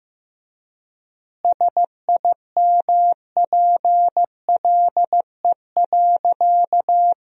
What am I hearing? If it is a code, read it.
Morse code, heard as a keyed tone, about 15 wpm, 700 Hz. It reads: SIMPLE.
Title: 4x Word-Spacing - Sentences from Top 500 Words - 15wpm